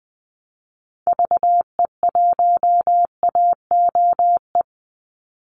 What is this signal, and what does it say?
Morse code, heard as a keyed tone, about 20 wpm, 700 Hz. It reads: VE1AOE
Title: Call Signs Repeated 3x - 20wpm